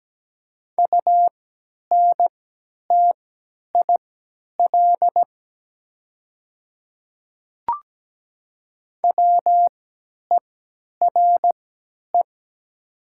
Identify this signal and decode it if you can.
Morse code, heard as a keyed tone, about 17 wpm, spaced as for 10 wpm, 700 Hz. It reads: UNTIL WERE
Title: 3x Character Spacing - Top 300 Words - 17wpm